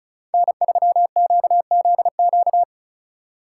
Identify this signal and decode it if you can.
Morse code, heard as a keyed tone, about 36 wpm, 700 Hz. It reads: N3QZQ
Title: USA Call Signs - 5 Characters - 4x Character Spacing - 35wpm